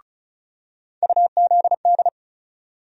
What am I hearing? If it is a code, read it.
Morse code, heard as a keyed tone, about 34 wpm, 700 Hz. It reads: UZD
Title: USA Call Sign Postfixes - 3 Letters 35wpm